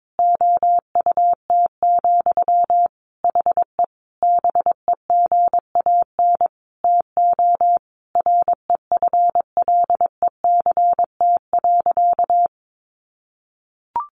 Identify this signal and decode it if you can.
Morse code, heard as a keyed tone, about 22 wpm, 700 Hz. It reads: OUT, HE BEGAN TO REFLECT.
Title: Sentences from Top 1500 Words 22wpm